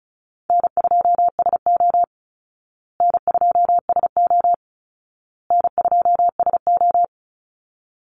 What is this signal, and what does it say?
Morse code, heard as a keyed tone, about 34 wpm, 700 Hz. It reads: N2SO N2SO N2SO
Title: Call Signs Repeated 2x - 35wpm